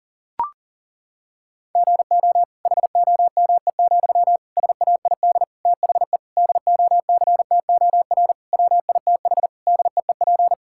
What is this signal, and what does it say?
Morse code, heard as a keyed tone, about 40 wpm, 700 Hz. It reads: GO HOME, SAID THE DOCTOR WITH DEEP
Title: Sentences from Top 1600 Words 40wpm